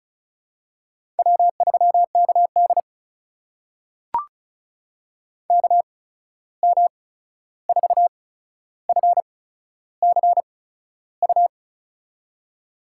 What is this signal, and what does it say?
Morse code, heard as a keyed tone, about 36 wpm, spaced as for 13 wpm, 700 Hz. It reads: W3KD KM4FCU
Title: USA Call Signs - 8x Character Spacing - 35wpm